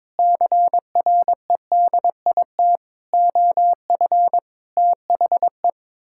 Morse code, heard as a keyed tone, about 22 wpm, 700 Hz. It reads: CREDIT OF THE
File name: practice / 22wpm / Sets of 7 Words - 22wpm